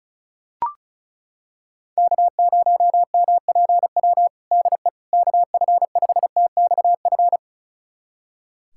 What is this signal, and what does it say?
Morse code, heard as a keyed tone, about 34 wpm, 700 Hz. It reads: K0MPW DE KF5TXF
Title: USA Callsign DE Callsign 35wpm